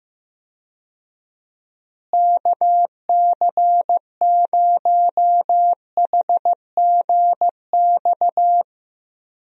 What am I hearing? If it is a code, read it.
Morse code, heard as a keyed tone, about 15 wpm, 700 Hz. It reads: KC0HGX
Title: USA Call Signs - 6 Characters - Repeated 3x - 15wpm